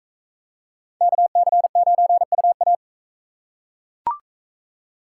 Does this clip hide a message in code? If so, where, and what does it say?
Yes: Morse code from 1.0 to 2.8 s, heard as a keyed tone, about 40 wpm, 700 Hz. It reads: KC9UA